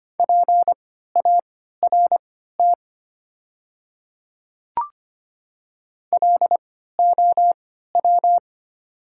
Morse code, heard as a keyed tone, about 25 wpm, spaced as for 14 wpm, 700 Hz. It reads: PART LOW
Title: 3x Character Spacing - Top 300 Words - 25wpm